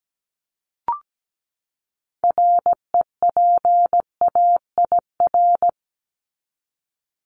Morse code, heard as a keyed tone, about 17 wpm, 700 Hz. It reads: REPAIR
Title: Top 1501-1750 Words 17wpm